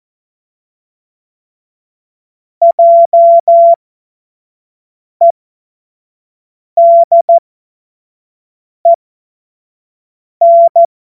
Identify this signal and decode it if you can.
Morse code, heard as a keyed tone, about 14 wpm, spaced as for 5 wpm, 680 Hz. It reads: JEDEN
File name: Morsekurs / Witze / Witz 23 April 2024 25 F70